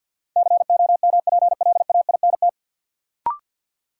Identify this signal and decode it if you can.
Morse code, heard as a keyed tone, about 52 wpm, 700 Hz. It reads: COMPLAINT
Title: Sentences from Top 1400 Words - New Word Drill - 50wpm